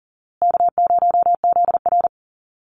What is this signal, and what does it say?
Morse code, heard as a keyed tone, about 40 wpm, 700 Hz. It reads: K0ZR
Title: Easy Call Signs Repeated 3x - 40wpm